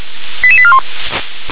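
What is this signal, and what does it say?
Five-tone selective call (ZVEI), heard as quick runs of five tones, tones of about 70 ms. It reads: 88841
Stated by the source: ZVEI five-tone selective call received with an SDR.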